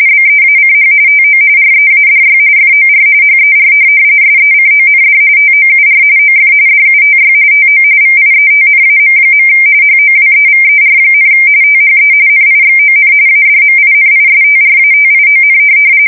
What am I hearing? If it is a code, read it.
RTTY, about 45.45 baud. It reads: BEACH TO READY FOR HER SECOND TOUR OF DUTY TO VIETNAM. HOWEVER, THE NIXON ADMINISTRATION
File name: fsk170 / USS-New-Jersey